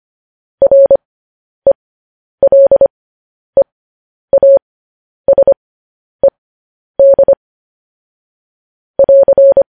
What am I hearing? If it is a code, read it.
Morse code, heard as a keyed tone, about 25 wpm, spaced as for 10 wpm, 550 Hz. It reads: RELEASED <AR>
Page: news headlines in morse code at 10 wpm